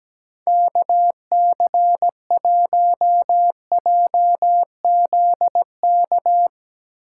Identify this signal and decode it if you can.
Morse code, heard as a keyed tone, about 17 wpm, 700 Hz. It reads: KC1JZK
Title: Speed Racing - USA Call Signs 17wpm